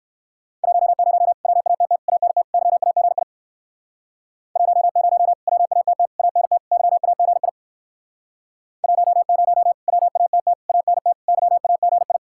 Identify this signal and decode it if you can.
Morse code, heard as a keyed tone, about 58 wpm, 700 Hz. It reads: 10 WATT ANT YAGI 10 WATT ANT YAGI 10 WATT ANT YAGI